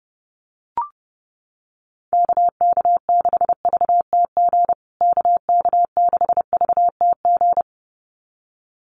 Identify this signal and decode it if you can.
Morse code, heard as a keyed tone, about 30 wpm, 700 Hz. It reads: KK6VTG KK6VTG
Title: USA Call Signs Repeated 2x 30wpm